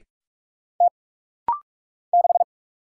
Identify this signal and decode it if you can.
Morse code, heard as a keyed tone, about 43 wpm, 700 Hz. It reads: T 6
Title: Single Letter-Number - Rapid-Fire 45wpm